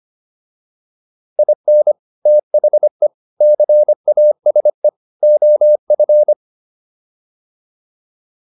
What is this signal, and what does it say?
Morse code, heard as a keyed tone, about 25 wpm, 600 Hz. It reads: IN THE CASE OF
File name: WD304 25